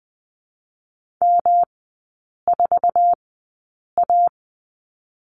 Morse code, heard as a keyed tone, about 20 wpm, spaced as for 8 wpm, 700 Hz. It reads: M4A
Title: 107 Lesson 18e - Letter m - Three Characters 20wpm